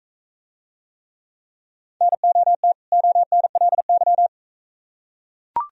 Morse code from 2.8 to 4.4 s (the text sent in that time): ONLY